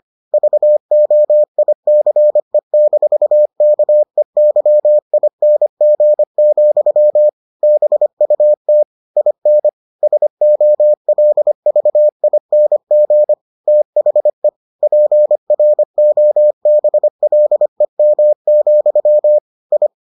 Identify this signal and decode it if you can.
Morse code, heard as a keyed tone, about 25 wpm, 600 Hz. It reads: VOICE-KEYING, BUT IN SOLVING THE PROBLEM, I